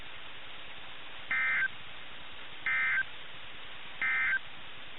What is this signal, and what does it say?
Emergency Alert System (SAME) tones, heard as harsh buzzing bursts. It is heard from 1.2 to 4.5 s.
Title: NOAA Weather Radio - KIH61